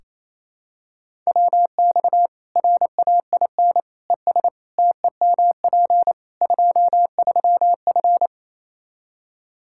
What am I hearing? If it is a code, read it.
Morse code, heard as a keyed tone, about 28 wpm, 700 Hz. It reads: WX RAIN ES TEMP 23F
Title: K1USN SST QSO Practice 28wpm